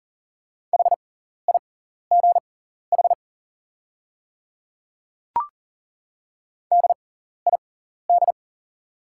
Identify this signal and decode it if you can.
Morse code, heard as a keyed tone, about 40 wpm, spaced as for 14 wpm, 690 Hz. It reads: HIGH DID
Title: 6x Character Spacing - Top 200 Words - 40wpm